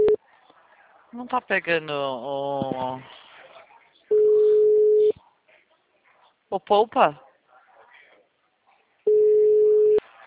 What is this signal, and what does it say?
Ringback tone (Europe, 425 Hz), with speech.